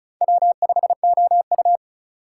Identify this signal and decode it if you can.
Morse code, heard as a keyed tone, about 34 wpm, 700 Hz. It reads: W5OU
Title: USA Call Signs - 4 Characters - 3x Character Spacing - 35wpm